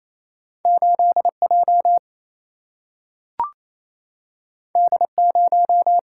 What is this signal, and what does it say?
Morse code, heard as a keyed tone, about 28 wpm, 700 Hz. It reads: 8 J D 0
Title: Single-Letter Single-Number 28wpm